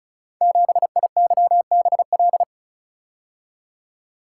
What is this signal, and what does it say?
Morse code, heard as a keyed tone, about 36 wpm, 700 Hz. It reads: 7IYBL
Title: Code Groups - 5 Letter:Numbers - 35wpm